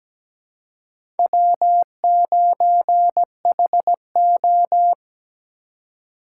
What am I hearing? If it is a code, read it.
Morse code, heard as a keyed tone, about 17 wpm, 700 Hz. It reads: W9HO
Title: USA Call Signs - 4x Character Spacing - 17wpm